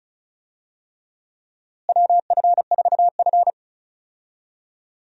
Morse code, heard as a keyed tone, about 36 wpm, 700 Hz. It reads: WF4F
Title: USA Call Signs - 4 Characters - Repeated 3x - 35wpm